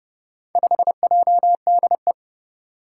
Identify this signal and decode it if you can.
Morse code, heard as a keyed tone, about 30 wpm, 700 Hz. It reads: 5JDE
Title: Four Challenging Letter-Number Combinations 30wpm